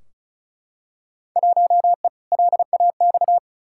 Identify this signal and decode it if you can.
Morse code, heard as a keyed tone, about 36 wpm, 700 Hz. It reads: 1E LAX
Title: Field Day Exchanges - 35wpm